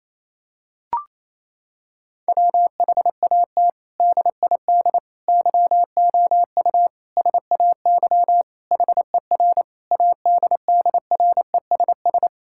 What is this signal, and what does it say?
Morse code, heard as a keyed tone, about 28 wpm, 700 Hz. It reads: WHAT DID YOU SAY HER ADDRESS
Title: Sentences from Top 1000 Words - Review Part 6 - 28wpm